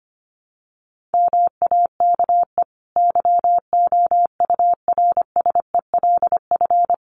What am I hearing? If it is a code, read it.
Morse code, heard as a keyed tone, about 25 wpm, 700 Hz. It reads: MAKE YOURSELF
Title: Sentences from Top 600 Words 25wpm